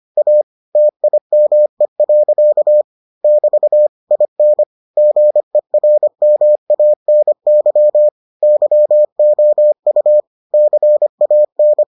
Morse code, heard as a keyed tone, about 25 wpm, 600 Hz. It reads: A TIME. <BT> IN GERMANY YOU CAN